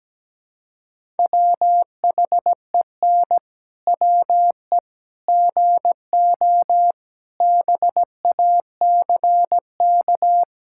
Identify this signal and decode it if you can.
Morse code, heard as a keyed tone, about 17 wpm, 700 Hz. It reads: WHEN WE GO BACK
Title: Pangram Challenge v2 - 17wpm